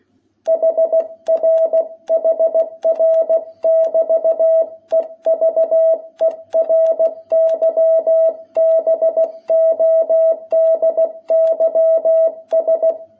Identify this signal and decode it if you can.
Morse code, heard as a keyed tone, about 16 wpm, 660 Hz. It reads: HR HR<BT>EVERYBODYS
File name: jh8cbh / 2021cw